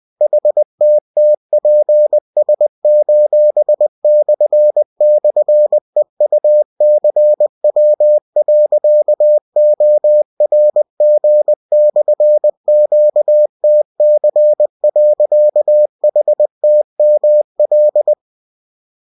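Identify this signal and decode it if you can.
Morse code, heard as a keyed tone, about 20 wpm, 600 Hz. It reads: HTTPS://EUCW.ORG/QTC.HTML
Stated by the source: Bulletin sent in CW.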